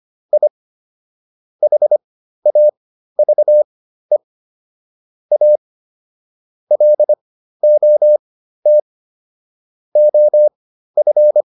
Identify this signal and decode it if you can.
Morse code, heard as a keyed tone, about 25 wpm, spaced as for 13 wpm, 600 Hz. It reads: I HAVE A LOT OF